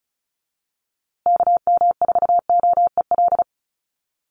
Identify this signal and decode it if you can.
Morse code, heard as a keyed tone, about 34 wpm, 700 Hz. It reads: KM4OEL
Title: USA Call Signs Repeated 2x 35wpm